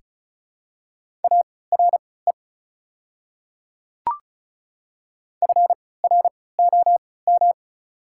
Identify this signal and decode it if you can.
Morse code, heard as a keyed tone, about 34 wpm, spaced as for 20 wpm, 690 Hz. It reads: ARE FROM
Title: 3x Character Spacing - Top 100 Words - 35wpm